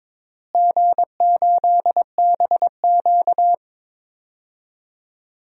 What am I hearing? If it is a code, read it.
Morse code, heard as a keyed tone, about 22 wpm, 700 Hz. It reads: G8BQ
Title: DXCC-100 Call Signs - 4 Characters - Repeated 3x - 22wpm